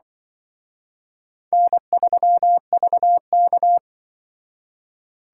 Morse code, heard as a keyed tone, about 24 wpm, 700 Hz. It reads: N3VK
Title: Speed Racing - USA Call Signs 20wpm